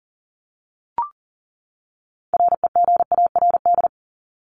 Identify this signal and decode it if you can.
Morse code, heard as a keyed tone, about 40 wpm, 700 Hz. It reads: REGARD
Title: ING Suffix - Top 100 Words with Base Words 40wpm